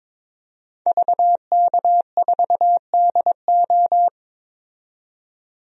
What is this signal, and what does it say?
Morse code, heard as a keyed tone, about 22 wpm, 700 Hz. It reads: VK4DO
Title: DXCC-100 Call Signs - 5 Characters - 22wpm